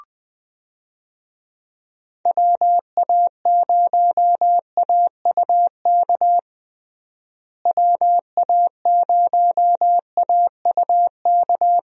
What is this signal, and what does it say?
Morse code, heard as a keyed tone, about 20 wpm, 700 Hz. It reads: WA0AUK WA0AUK
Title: USA Call Signs - 6 Characters - Repeated 2x - 20wpm